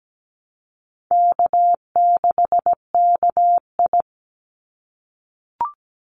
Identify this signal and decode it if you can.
Morse code, heard as a keyed tone, about 17 wpm, 700 Hz. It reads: K6KI
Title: Call Signs - 17wpm